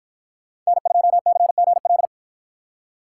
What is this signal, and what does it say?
Morse code, heard as a keyed tone, about 50 wpm, 700 Hz. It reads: N2CGL